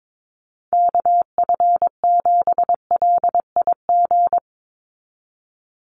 Morse code, heard as a keyed tone, about 22 wpm, 700 Hz. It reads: KF7LIG
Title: USA Call Signs 22wpm